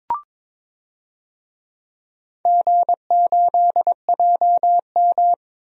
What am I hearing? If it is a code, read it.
Morse code, heard as a keyed tone, about 22 wpm, 700 Hz. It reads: G8JM
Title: DXCC-100 Call Signs - 4 Characters - 22wpm